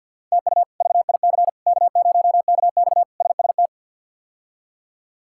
Morse code, heard as a keyed tone, about 53 wpm, 700 Hz. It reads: TU VIC K0KX SST
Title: K1USN SST QSO Practice - 50wpm